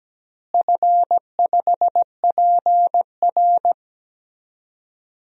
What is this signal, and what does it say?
Morse code, heard as a keyed tone, about 17 wpm, 700 Hz. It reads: F5PR
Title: DXCC-100 Call Signs - 4 Characters - Repeated 3x - 17wpm